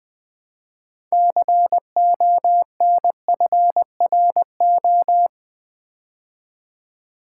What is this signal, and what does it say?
Morse code, heard as a keyed tone, about 20 wpm, 700 Hz. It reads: CONFRO
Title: Build A Word - 2 Letters At A Time - 8-Letter Words - 20wpm